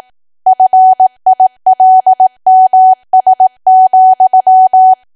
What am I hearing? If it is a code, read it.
Morse code, heard as a keyed tone, about 18 wpm, 750 Hz. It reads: FILMS,
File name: Archive / 18 WPM / 18WPM